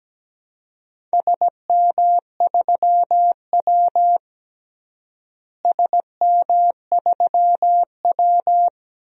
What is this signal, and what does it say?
Morse code, heard as a keyed tone, about 17 wpm, 700 Hz. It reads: SM3W SM3W
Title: DXCC-100 Call Signs - 4 Characters - Repeated 2x - 17wpm